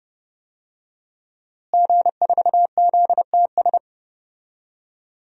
Morse code, heard as a keyed tone, about 30 wpm, 700 Hz. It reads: G4ZTS
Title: DXCC-100 Call Signs - 5 Characters - Repeated 3x - 30wpm